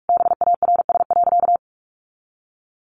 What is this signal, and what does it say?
Morse code, heard as a keyed tone, about 44 wpm, 700 Hz. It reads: BARS.
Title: Easy Phrases 45wpm